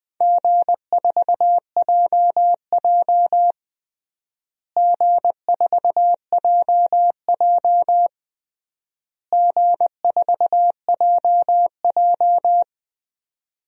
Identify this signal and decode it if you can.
Morse code, heard as a keyed tone, about 21 wpm, 700 Hz. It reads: G4JJ G4JJ G4JJ
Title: DXCC-100 Call Signs - 4 Characters - Repeated 3x - 20wpm